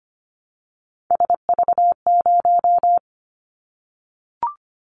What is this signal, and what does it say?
Morse code, heard as a keyed tone, about 25 wpm, 700 Hz. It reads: SV0
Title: Callsign Prefixes for Top 50 DXCC - 25wpm